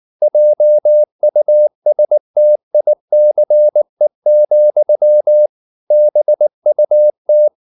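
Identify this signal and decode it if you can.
Morse code, heard as a keyed tone, about 19 wpm, 600 Hz. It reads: JUSTICE, BUT
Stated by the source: Morse code at 19 wpm from CW med Gnister, reading Ken Follett's novel Never in English.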